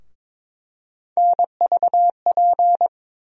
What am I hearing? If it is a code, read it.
Morse code, heard as a keyed tone, about 22 wpm, 700 Hz. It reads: NVP
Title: USA Call Sign Postfixes - 3 Letters 22wpm